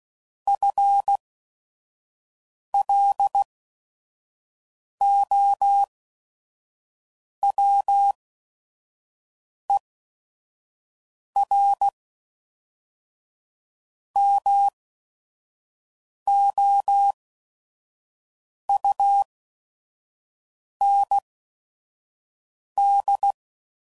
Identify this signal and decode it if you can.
Morse code, heard as a keyed tone, about 16 wpm, spaced as for 5 wpm, 800 Hz. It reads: FLOWER MOUND
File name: CW Code Groups @5WPM / fromSM's / Cities